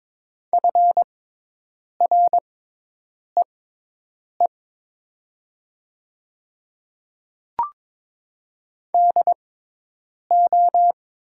Morse code, heard as a keyed tone, about 22 wpm, spaced as for 8 wpm, 700 Hz. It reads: FREE DO
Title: 6x Character Spacing - Top 500 Words - 22wpm